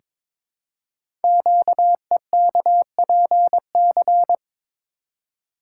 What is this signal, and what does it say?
Morse code, heard as a keyed tone, about 22 wpm, 700 Hz. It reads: QEKPC
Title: Code Groups - 5 Letters - 22wpm